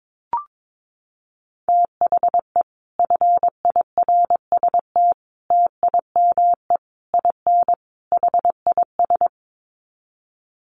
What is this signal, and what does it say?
Morse code, heard as a keyed tone, about 22 wpm, 700 Hz. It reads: THE FIRST TIME IN HIS
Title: Sets of 5 Words Encore2 22wpm